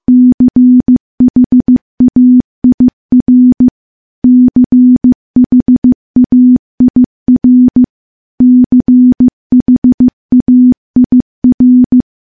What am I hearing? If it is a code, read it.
Morse code, heard as a keyed tone, about 15 wpm, 250 Hz. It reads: CHAIR CHAIR CHAIR
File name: morse-training-15WPM